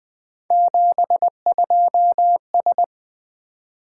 Morse code, heard as a keyed tone, about 20 wpm, 700 Hz. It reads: 72S
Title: Three Letter-Number Combinations - 20wpm